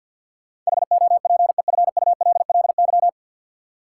Morse code, heard as a keyed tone, about 53 wpm, 700 Hz. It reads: HOPEFULLY
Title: Sentences from Top 1500 Words - New Word Drill - 50wpm